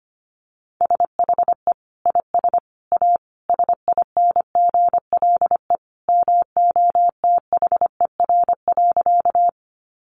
Morse code, heard as a keyed tone, about 25 wpm, 700 Hz. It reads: SHE IS A SINGLE MOTHER.
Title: Sentences from Top 700 Words 25wpm